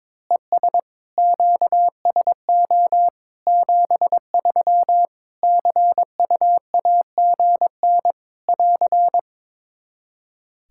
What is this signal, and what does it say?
Morse code, heard as a keyed tone, about 22 wpm, 700 Hz. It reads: ES QSO 73 CUAGN <AR>
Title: Practice QSO - Exchange 4 - Ending - 22wpm